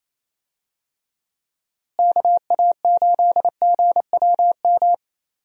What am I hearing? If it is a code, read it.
Morse code, heard as a keyed tone, about 27 wpm, 700 Hz. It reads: KA8GWM
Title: USA Call Signs - 6 Characters - Repeated 2x - 28wpm